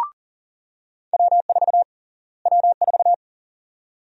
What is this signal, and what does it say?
Morse code, heard as a keyed tone, about 37 wpm, 700 Hz. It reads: W4 W4
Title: USA Call Sign Prefixes - Repeated 2x - 40wpm